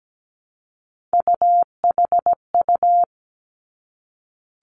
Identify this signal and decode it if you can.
Morse code, heard as a keyed tone, about 17 wpm, 700 Hz. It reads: UHU
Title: Three Challenging Letter-Number Combinations 17wpm